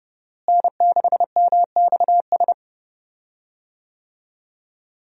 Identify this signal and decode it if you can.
Morse code, heard as a keyed tone, about 32 wpm, 700 Hz. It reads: N6MXS
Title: USA Call Signs - 5 Characters - Repeated 3x - 30wpm